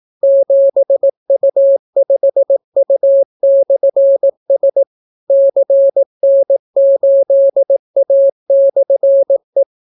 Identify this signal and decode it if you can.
Morse code, heard as a keyed tone, about 18 wpm, 550 Hz. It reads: 7U5U/S CN8A/E